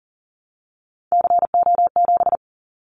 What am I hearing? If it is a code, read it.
Morse code, heard as a keyed tone, about 40 wpm, 700 Hz. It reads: CO7
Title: Callsign Prefixes for Top 100 DXCC - 40wpm